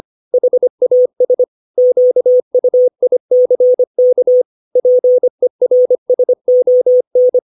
Morse code, heard as a keyed tone, about 25 wpm, 500 Hz. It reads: HAS QUICK PERSON
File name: 632 words / 632 Words 25WPM